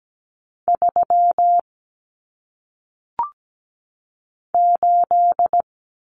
Morse code, heard as a keyed tone, about 17 wpm, 700 Hz. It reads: 3 8
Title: Common Cut Numbers - 1-Digit 17wpm